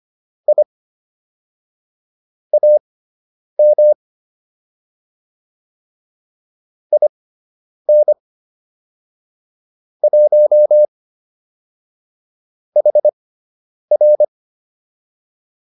Morse code, heard as a keyed tone, about 25 wpm, spaced as for 9 wpm, 600 Hz. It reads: I AM IN 1 HR